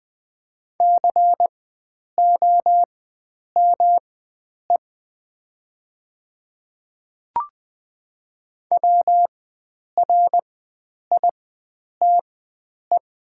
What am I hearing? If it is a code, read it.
Morse code, heard as a keyed tone, about 20 wpm, spaced as for 9 wpm, 700 Hz. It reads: COME WRITE
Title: 4x Character Spacing - Top 300 Words - 20wpm